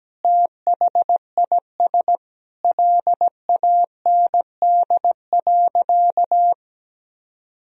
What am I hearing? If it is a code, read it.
Morse code, heard as a keyed tone, about 17 wpm, 700 Hz. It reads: THIS LAND.
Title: Sentences from Top 1000 Words - Review Part 4 - 17wpm